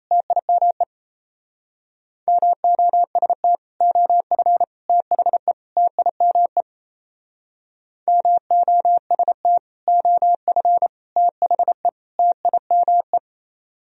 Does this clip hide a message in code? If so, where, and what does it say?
Yes: Morse code from 0.1 to 13.2 s, heard as a keyed tone, about 36 wpm, 700 Hz. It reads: TIME MOST OF THE TIME MOST OF THE TIME